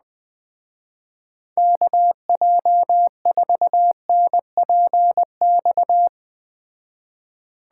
Morse code, heard as a keyed tone, about 19 wpm, 700 Hz. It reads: KJ4NPX
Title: Speed Racing - USA 2x3 Call Signs 17wpm